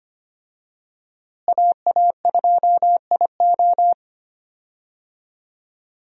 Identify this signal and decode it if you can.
Morse code, heard as a keyed tone, about 25 wpm, 700 Hz. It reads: AA2IO
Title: USA Call Signs - 5 Characters - Repeated 3x - 25wpm